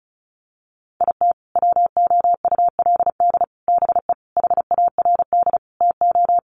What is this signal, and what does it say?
Morse code, heard as a keyed tone, about 34 wpm, 700 Hz. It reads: IT WOULD BE HARD TO
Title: Sets of 5 Words Encore2 35wpm